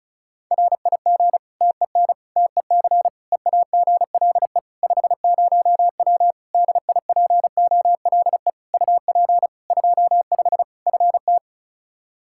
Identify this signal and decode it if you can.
Morse code, heard as a keyed tone, about 36 wpm, 700 Hz. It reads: RIG TEN TEC EAGLE 50W DIPOLE UP 25 FT
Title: Practice QSO - Exchange 2 - Info - 35wpm